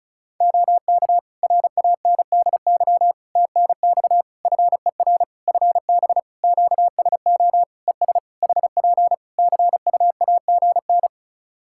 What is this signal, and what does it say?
Morse code, heard as a keyed tone, about 36 wpm, 700 Hz. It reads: OK RANDY TNX FER FB QSO ES HP CUAGN
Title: Practice QSO - Exchange 4 - Ending - 35wpm